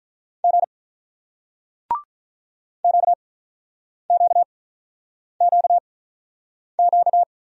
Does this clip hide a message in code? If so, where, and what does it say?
Yes: Morse code from 0.4 to 7.2 s, heard as a keyed tone, about 47 wpm, spaced as for 9 wpm, 700 Hz. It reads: G QQQQ